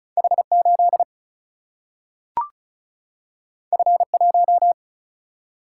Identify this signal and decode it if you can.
Morse code, heard as a keyed tone, about 36 wpm, 700 Hz. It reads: H8 F1
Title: Single-Letter Single-Number 35wpm